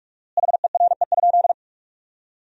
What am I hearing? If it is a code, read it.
Morse code, heard as a keyed tone, about 45 wpm, 700 Hz. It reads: HERE?
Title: 4x Word-Spacing - Sentences from Top 500 Words - 45wpm